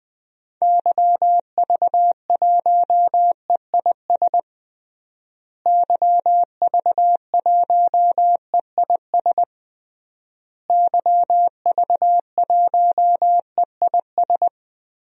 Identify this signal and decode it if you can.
Morse code, heard as a keyed tone, about 20 wpm, 700 Hz. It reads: YV1EIS YV1EIS YV1EIS
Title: DXCC-100 Call Signs - 6 Characters - Repeated 3x - 20wpm